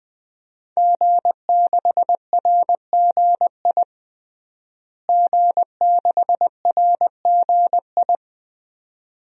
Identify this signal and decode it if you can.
Morse code, heard as a keyed tone, about 20 wpm, 700 Hz. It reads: G6RGI G6RGI
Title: DXCC-100 Call Signs - 5 Characters - Repeated 2x - 20wpm